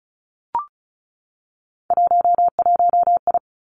Morse code, heard as a keyed tone, about 36 wpm, 700 Hz. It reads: 11I
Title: Three Challenging Letter-Number Combinations 35wpm